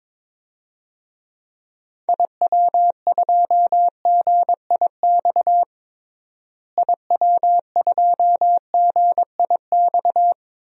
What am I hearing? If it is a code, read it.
Morse code, heard as a keyed tone, about 21 wpm, 700 Hz. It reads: IW2GIX IW2GIX
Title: DXCC-100 Call Signs - 6 Characters - Repeated 2x - 22wpm